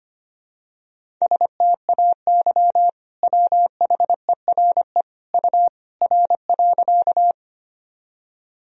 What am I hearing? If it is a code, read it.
Morse code, heard as a keyed tone, about 25 wpm, 700 Hz. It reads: STAY WHERE U R.
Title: Sentences from Top 500 Words and Abbreviations 25wpm